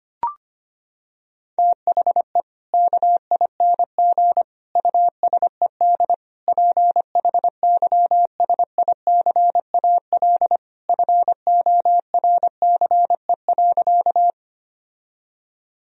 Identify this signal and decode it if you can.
Morse code, heard as a keyed tone, about 25 wpm, 700 Hz. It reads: THE KING USED PHYSICAL FORCE.
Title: Sentences from Top 1000 Words - Review Part 4 - 25wpm